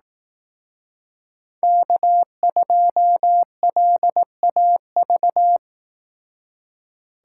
Morse code, heard as a keyed tone, about 18 wpm, 700 Hz. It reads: K2LAV
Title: Speed Racing - USA 1x3 Call Signs 15wpm